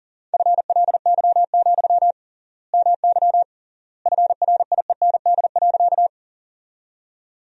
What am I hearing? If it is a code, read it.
Morse code, heard as a keyed tone, about 40 wpm, 700 Hz. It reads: FLY, MY FRIEND.
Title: 3x Word-Spacing - Sentences from Top 500 Words - 40wpm